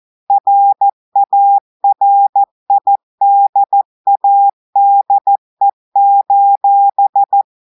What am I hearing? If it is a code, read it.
Morse code, heard as a keyed tone, about 14 wpm, 800 Hz. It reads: RARIDADE:
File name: audios / portugais / portugais texte en clair 1 vitesse 14 mots minutes